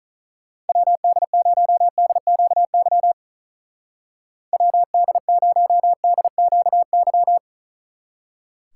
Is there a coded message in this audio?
Yes: Morse code from 0.7 to 7.4 s, heard as a keyed tone, about 44 wpm, 700 Hz. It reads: WD0DQY WD0DQY